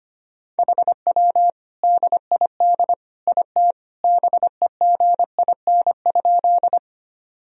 Morse code, heard as a keyed tone, about 25 wpm, 700 Hz. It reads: HW DID IT BEGIN?
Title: Sentences from Top 300 Words and Abbreviations 25wpm